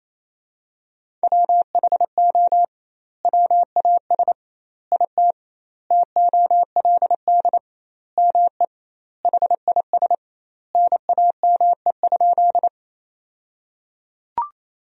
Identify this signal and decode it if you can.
Morse code, heard as a keyed tone, about 28 wpm, 700 Hz. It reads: WHO WAS IT TOLD ME HIS NAME?
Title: 2x Word-Spacing - Sentences from Top 400 Words - 28wpm